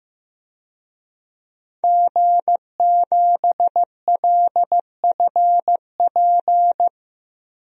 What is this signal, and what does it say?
Morse code, heard as a keyed tone, about 15 wpm, 700 Hz. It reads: G7LFP
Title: DXCC-100 Call Signs - 5 Characters - Repeated 3x - 15wpm